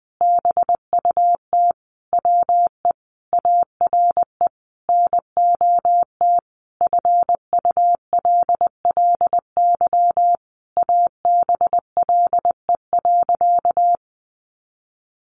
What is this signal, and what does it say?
Morse code, heard as a keyed tone, about 20 wpm, 700 Hz. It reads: BUT WE ARE NOT FULLY ABLE.
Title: Sentences from Top 1300 Words 20wpm